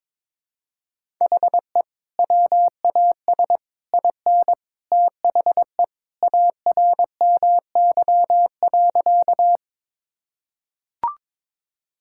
Morse code, heard as a keyed tone, about 22 wpm, 700 Hz. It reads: HE WAS IN THE ARMY.
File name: practice / 22wpm / Sentences with 3-Word Phrases - 22wpm